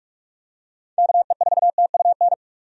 Morse code, heard as a keyed tone, about 46 wpm, 700 Hz. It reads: KE4TUN
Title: USA Call Signs - 6 Characters - 45wpm